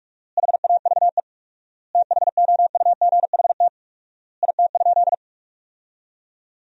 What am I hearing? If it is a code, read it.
Morse code, heard as a keyed tone, about 48 wpm, 700 Hz. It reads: HAVE THOUGHT IT?
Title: 4x Word-Spacing - Sentences from Top 300 Words - 45wpm